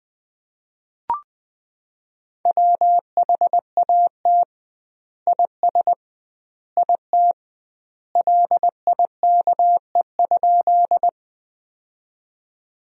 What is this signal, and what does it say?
Morse code, heard as a keyed tone, about 20 wpm, 700 Hz. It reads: WHAT IS IT LIKE?
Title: 2x Word-Spacing - Sentences from Top 100 Words - 20wpm